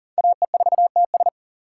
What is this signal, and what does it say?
Morse code, heard as a keyed tone, about 40 wpm, 700 Hz. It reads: AE4TS
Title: USA Call Signs - 5 Characters - 2x Character Spacing - 40wpm